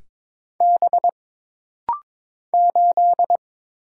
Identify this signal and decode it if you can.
Morse code, heard as a keyed tone, about 22 wpm, 700 Hz. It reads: B 8
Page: This rapid-fire timing is geared to push your ICR (Instant Character Recognition) skills.